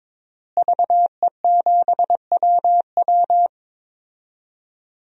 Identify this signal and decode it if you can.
Morse code, heard as a keyed tone, about 22 wpm, 700 Hz. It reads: VE7WW
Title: DXCC-100 Call Signs - 5 Characters - Repeated 3x - 22wpm